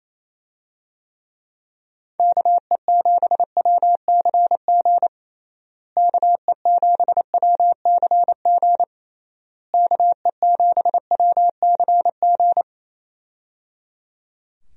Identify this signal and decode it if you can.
Morse code, heard as a keyed tone, about 27 wpm, 700 Hz. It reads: KE7WCG KE7WCG KE7WCG
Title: USA Call Signs - 6 Characters - Repeated 3x - 28wpm